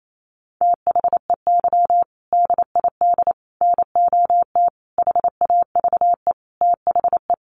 Morse code, heard as a keyed tone, about 28 wpm, 700 Hz. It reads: THEY DID NOT HAVE THE
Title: Sentences from Top 1400 Words 28wpm